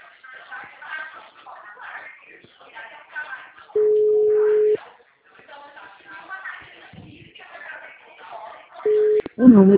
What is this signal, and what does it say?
Ringback tone (Europe, 425 Hz), with speech.
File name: out- -2007- - - .25426